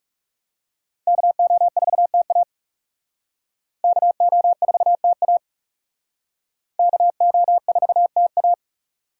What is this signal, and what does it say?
Morse code, heard as a keyed tone, about 37 wpm, 700 Hz. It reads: KO4TA KO4TA KO4TA
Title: Speed Racing - USA Call Signs 30wpm